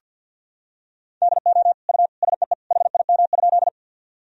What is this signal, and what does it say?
Morse code, heard as a keyed tone, about 50 wpm, 700 Hz. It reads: DO U SEE HIM?
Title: Sentences from Top 100 Words and Abbreviations 50wpm